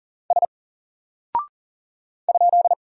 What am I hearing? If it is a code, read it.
Morse code, heard as a keyed tone, about 40 wpm, 700 Hz. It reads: S ?
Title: Single Character (Letter:Number:Punctuation:Prosign) 40wpm